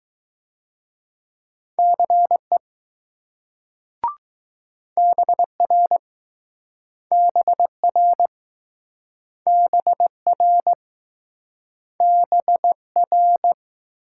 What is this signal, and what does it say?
Morse code, heard as a keyed tone, about 20 wpm, 700 Hz. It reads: CE BR BR BR BR